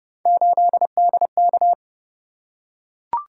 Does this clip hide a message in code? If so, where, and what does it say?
Yes: Morse code from 0.3 to 1.7 s, heard as a keyed tone, about 28 wpm, 700 Hz. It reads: 8DK